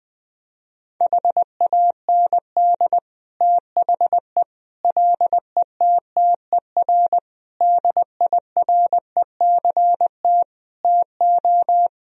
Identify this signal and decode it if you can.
Morse code, heard as a keyed tone, about 21 wpm, 700 Hz. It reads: HAND THE LETTER DIRECT TO
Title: Sentences from Top 1000 Words - Review Part 2 - 20wpm